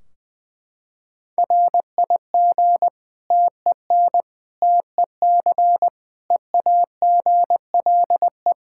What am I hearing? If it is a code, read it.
Morse code, heard as a keyed tone, about 20 wpm, 700 Hz. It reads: RIG TEN TEC EAGLE
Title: Practice QSO - Exchange 2 - Info - 20wpm